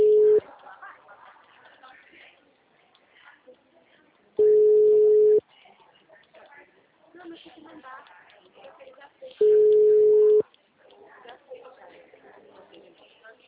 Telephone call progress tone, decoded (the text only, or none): ringback tone